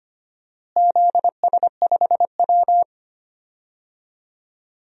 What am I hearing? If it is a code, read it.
Morse code, heard as a keyed tone, about 25 wpm, 700 Hz. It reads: ZS5W
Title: DXCC-100 Call Signs - 4 Characters - Repeated 2x - 25wpm